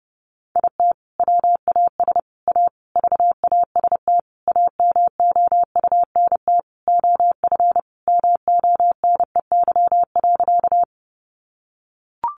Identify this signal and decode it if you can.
Morse code, heard as a keyed tone, about 30 wpm, 700 Hz. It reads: IT WAS A VAST AMOUNT OF MONEY.
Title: Sentences from Top 1300 Words 30wpm